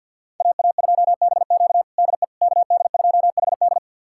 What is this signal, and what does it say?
Morse code, heard as a keyed tone, about 48 wpm, 700 Hz. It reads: AA2BQ DE KD2HD
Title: USA Callsign DE Callsign - 5 Characters - 50wpm